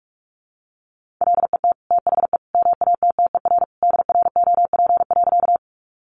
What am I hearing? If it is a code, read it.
Morse code, heard as a keyed tone, about 45 wpm, 700 Hz. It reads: LET THE MATTER DROP.
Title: Sentences from Top 1400 Words 45wpm